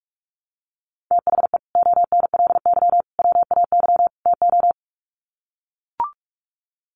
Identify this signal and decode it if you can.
Morse code, heard as a keyed tone, about 45 wpm, 700 Hz. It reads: THE ONLY WAY TO
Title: Sets of 4 Words 45wpm